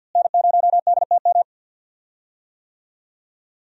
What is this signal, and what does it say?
Morse code, heard as a keyed tone, about 52 wpm, 700 Hz. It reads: N0DTM
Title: USA Call Signs - 5 Characters - Repeated 3x - 50wpm